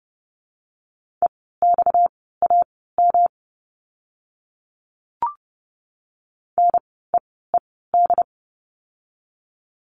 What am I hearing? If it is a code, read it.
Morse code, heard as a keyed tone, about 30 wpm, spaced as for 17 wpm, 700 Hz. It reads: EXAM NEED
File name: course / 30:10wpm / 230 Lesson 37f - Letter x - Words 30wpm